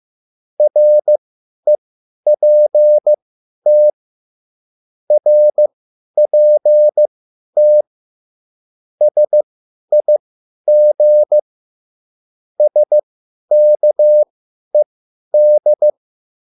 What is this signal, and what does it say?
Morse code, heard as a keyed tone, about 15 wpm, spaced as for 10 wpm, 600 Hz. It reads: REPT RPT SIG SKED
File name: QSL / CW lesson